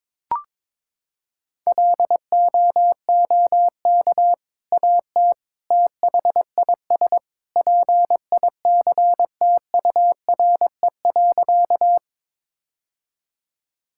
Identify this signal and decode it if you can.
Morse code, heard as a keyed tone, about 21 wpm, 700 Hz. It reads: LOOK AT THIS PICTURE.
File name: practice / 22wpm / Sentences with 2-Word Phrases - 22wpm